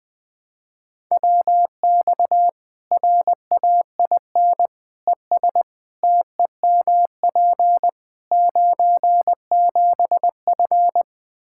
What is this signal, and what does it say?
Morse code, heard as a keyed tone, about 20 wpm, 700 Hz. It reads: WX RAIN ES TEMP 97F
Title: Practice QSO - Exchange 2 - Info - 20wpm